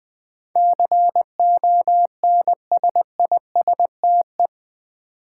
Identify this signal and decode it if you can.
Morse code, heard as a keyed tone, about 20 wpm, 700 Hz. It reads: CONSISTE
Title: Build A Word - 4 Letters At A Time - 12-Letter Words - 20wpm